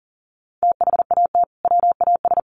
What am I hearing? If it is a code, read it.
Morse code, heard as a keyed tone, about 40 wpm, 700 Hz. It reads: THAT WAS
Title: Sets of 2 Words 40wpm